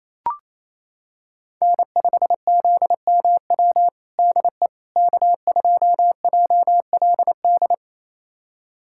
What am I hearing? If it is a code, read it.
Morse code, heard as a keyed tone, about 28 wpm, 700 Hz. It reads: N5ZMW DE K2JLD
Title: USA Callsign DE Callsign - 5 Characters - 28wpm